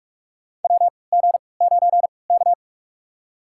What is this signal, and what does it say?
Morse code, heard as a keyed tone, about 43 wpm, spaced as for 25 wpm, 700 Hz. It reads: WG9K
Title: USA Call Signs - 3x Character Spacing - 45wpm